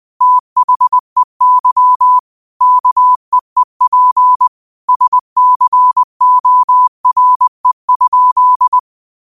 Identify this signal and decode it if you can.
Morse code, heard as a keyed tone, about 20 wpm, 1000 Hz. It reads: THEY KEEP SCORE?
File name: 20wpm